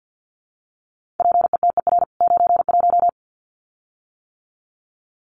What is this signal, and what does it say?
Morse code, heard as a keyed tone, about 48 wpm, 700 Hz. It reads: PETER 91
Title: CWT Exchange 50wpm